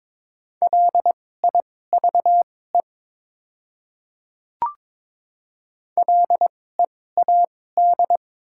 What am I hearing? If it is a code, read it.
Morse code, heard as a keyed tone, about 22 wpm, spaced as for 16 wpm, 700 Hz. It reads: LIVE LEAD
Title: 2x Character Spacing - Top 500 Words - 22wpm